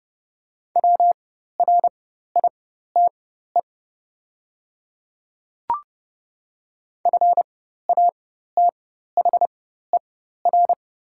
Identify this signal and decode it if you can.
Morse code, heard as a keyed tone, about 30 wpm, spaced as for 14 wpm, 700 Hz. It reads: WRITE FATHER
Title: 4x Character Spacing - Top 200 Words - 30wpm